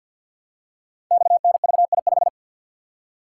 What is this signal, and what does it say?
Morse code, heard as a keyed tone, about 50 wpm, 700 Hz. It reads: XNVI5
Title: Code Groups - 5 Letter:Numbers - 50wpm